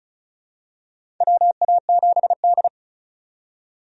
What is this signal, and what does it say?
Morse code, heard as a keyed tone, about 34 wpm, 700 Hz. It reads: WA7D
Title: USA Call Signs - 4 Characters - 35wpm